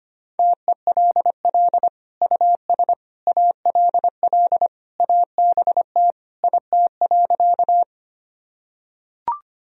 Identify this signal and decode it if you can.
Morse code, heard as a keyed tone, about 25 wpm, 700 Hz. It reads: TELL US ALL ABT IT.
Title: Sentences from Top 200 Words and Abbreviations 25wpm